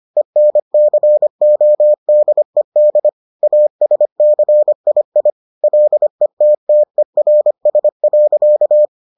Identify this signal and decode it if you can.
Morse code, heard as a keyed tone, about 25 wpm, 600 Hz. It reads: ENCODED ASCII LETTERS.